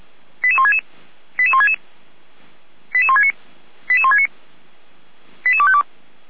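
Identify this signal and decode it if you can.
Five-tone selective call (ZVEI), heard as quick runs of five tones, tones of about 70 ms. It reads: 88166 88166 88169 88169 88252
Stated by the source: ZVEI five-tone selective call 88352 received with an SDR.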